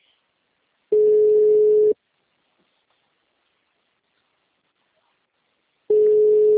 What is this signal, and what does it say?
Ringback tone (Europe, 425 Hz).
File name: out- -2051- - - .44939